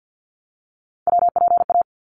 Morse code, heard as a keyed tone, about 50 wpm, 700 Hz. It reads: WPA
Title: ARRL Sections 50wpm